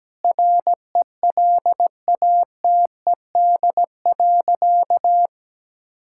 Morse code, heard as a keyed tone, about 17 wpm, 700 Hz. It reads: RELATED.
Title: Sentences from Top 1000 Words - Review Part 5 - 17wpm